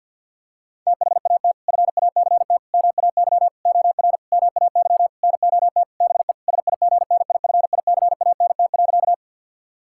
Morse code, heard as a keyed tone, about 51 wpm, 700 Hz. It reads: THAT FACT MAY OR MAY NOT BE SIGNIFICANT.
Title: Sentences from Top 1000 Words - Review Part 6 - 50wpm